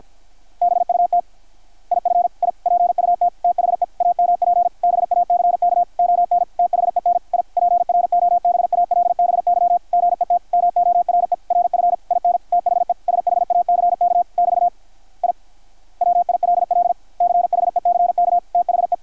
Morse code, heard as a keyed tone, about 53 wpm, 720 Hz, with radio noise. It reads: BUT IF I PUT THE AMP BACK ON THEN I PROBALBY GET MORE RF IN THE SHACK <BT> I WILL CHECK THE
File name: archived / CRSnet